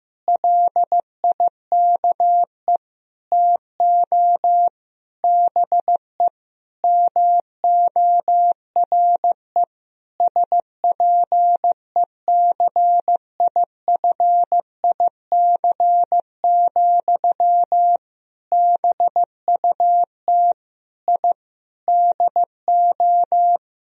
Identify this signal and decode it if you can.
Morse code, heard as a keyed tone, about 15 wpm, 700 Hz. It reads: LIKE TO BE MORE SPECIFIC, BUT I DO
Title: Sentences from Top 1000 Words - Review Part 3 - 15wpm